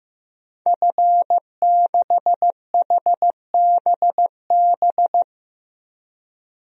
Morse code, heard as a keyed tone, about 15 wpm, 700 Hz. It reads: F6HBB
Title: DXCC-100 Call Signs - 5 Characters - Repeated 2x - 15wpm